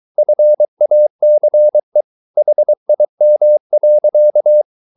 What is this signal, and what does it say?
Morse code, heard as a keyed tone, about 23 wpm, 600 Hz. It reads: FACE HIM.